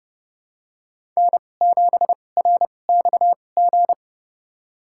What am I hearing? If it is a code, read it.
Morse code, heard as a keyed tone, about 31 wpm, spaced as for 22 wpm, 700 Hz. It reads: N7RXG
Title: USA Call Signs - 5 Characters - 2x Character Spacing - 30wpm